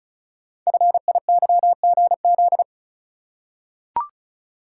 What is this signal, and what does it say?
Morse code, heard as a keyed tone, about 34 wpm, 700 Hz. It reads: FIYGZ
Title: Code Groups - 5 Letters - 35wpm